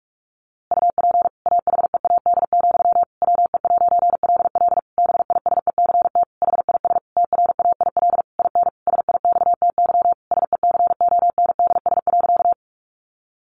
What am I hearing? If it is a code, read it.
Morse code, heard as a keyed tone, about 44 wpm, 700 Hz. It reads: UP AHEAD, WE'LL BISECT HIS TRAIL IN SIXTY SECONDS.
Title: A Sound Of Thunder 45wpm